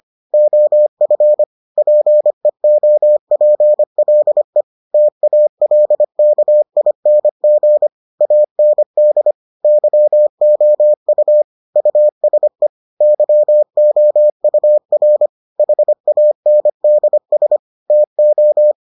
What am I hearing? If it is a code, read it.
Morse code, heard as a keyed tone, about 25 wpm, 600 Hz. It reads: OF PEOPLE TALKING AND YOU USE YOUR HANDS TO